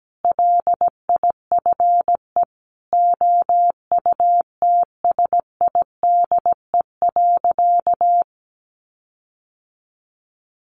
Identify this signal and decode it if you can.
Morse code, heard as a keyed tone, about 17 wpm, 700 Hz. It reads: LIFE OUTSIDE.
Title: Sentences from Top 1300 Words 17wpm